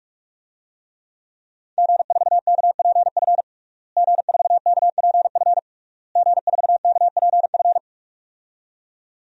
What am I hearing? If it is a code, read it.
Morse code, heard as a keyed tone, about 46 wpm, 700 Hz. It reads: G4KPF G4KPF G4KPF
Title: DXCC-100 Call Signs - 5 Characters - Repeated 3x - 45wpm